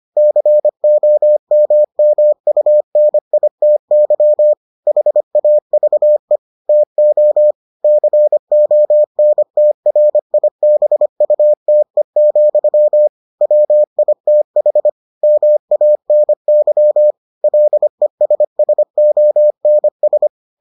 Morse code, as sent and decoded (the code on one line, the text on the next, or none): -.-. --- -- -- ..- -. .. - -.-- / .... .- ...- . / - --- / -.-. --- -. - .-. .. -... ..- - . --..-- / .-- .. - .... / -- .- -. -.-- / .-.. ... ... --- -. ...
COMMUNITY HAVE TO CONTRIBUTE, WITH MANY LESSONS